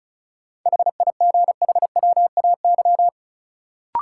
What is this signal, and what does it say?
Morse code, heard as a keyed tone, about 36 wpm, 700 Hz. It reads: HIGHWAY
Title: Sentences from Top 1400 Words - New Word Drill - 35wpm